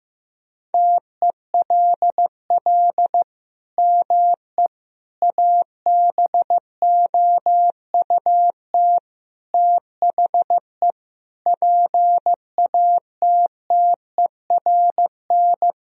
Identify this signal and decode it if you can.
Morse code, heard as a keyed tone, about 15 wpm, 700 Hz. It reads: TELL ME ABOUT THE PATTERN
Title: Sentences from Top 1000 Words - Review Part 2 - 15wpm